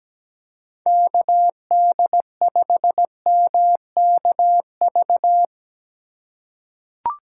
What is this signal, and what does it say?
Morse code, heard as a keyed tone, about 17 wpm, 700 Hz. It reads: KD5MKV
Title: USA Call Signs - 6 Characters - 2x Character Spacing - 17wpm